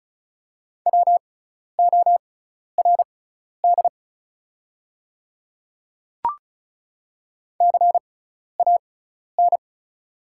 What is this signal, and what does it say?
Morse code, heard as a keyed tone, about 36 wpm, spaced as for 12 wpm, 700 Hz. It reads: WORD CAN
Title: 6x Character Spacing - Top 300 Words - 35wpm